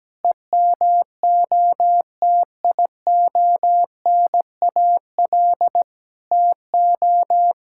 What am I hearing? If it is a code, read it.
Morse code, heard as a keyed tone, about 17 wpm, 700 Hz. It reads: EMOTIONAL TO
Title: Sentences from Top 1000 Words - Review Part 5 - 17wpm